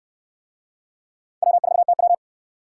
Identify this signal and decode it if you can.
Morse code, heard as a keyed tone, about 69 wpm, 700 Hz. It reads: W4TW